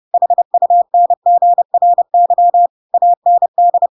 Morse code, heard as a keyed tone, about 30 wpm, 700 Hz. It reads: HUNGRY AND